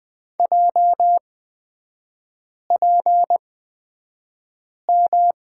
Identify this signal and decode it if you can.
Morse code, heard as a keyed tone, about 20 wpm, spaced as for 5 wpm, 690 Hz. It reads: JPM
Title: 236 Lesson 38e - Letter q - Three Characters 20wpm